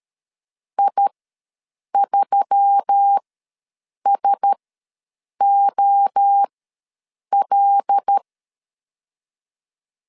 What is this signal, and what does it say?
Morse code, heard as a keyed tone, about 13 wpm, spaced as for 7 wpm, 800 Hz. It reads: I3SOL
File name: LEZIONI / 5 LBVC / 5 LB ws 12wpm 3034